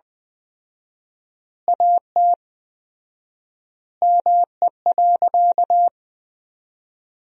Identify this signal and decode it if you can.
Morse code, heard as a keyed tone, about 20 wpm, 700 Hz. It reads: AT ME.
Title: 4x Word-Spacing - Sentences from Top 200 Words - 20wpm